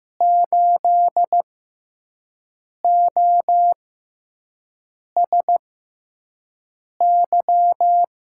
Morse code, heard as a keyed tone, about 15 wpm, spaced as for 5 wpm, 700 Hz. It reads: 8OSY
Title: USA Call Signs - 6x Character Spacing - 15wpm